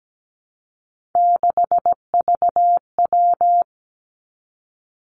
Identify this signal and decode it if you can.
Morse code, heard as a keyed tone, about 17 wpm, 700 Hz. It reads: 6VW
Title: Three Challenging Letter-Number Combinations 17wpm